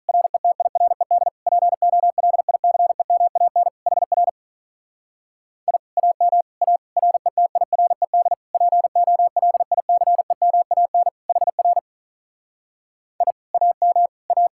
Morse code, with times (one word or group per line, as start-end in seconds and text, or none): 0.1-1.3 RETIRED
1.5-3.7 POLICEMAN
3.9-4.3 HR
5.7-5.8 I
6.0-6.4 AM
6.6-6.8 A
7.0-8.3 RETIRED
8.5-11.1 POLICEMAN
11.3-11.8 HR
13.2-13.3 I
13.5-14.1 AM
14.3-14.5 A